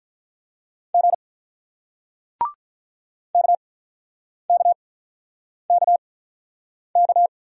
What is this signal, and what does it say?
Morse code, heard as a keyed tone, about 46 wpm, spaced as for 9 wpm, 700 Hz. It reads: G KKKK